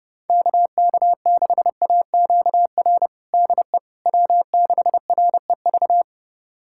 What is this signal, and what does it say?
Morse code, heard as a keyed tone, about 30 wpm, 700 Hz. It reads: KK6AQR DE W6REV
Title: USA Callsign DE Callsign 30wpm